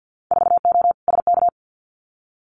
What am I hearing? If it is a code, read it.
Morse code, heard as a keyed tone, about 50 wpm, 700 Hz. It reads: 4O SK
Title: Winter Field Day Exchanges Encore 50wpm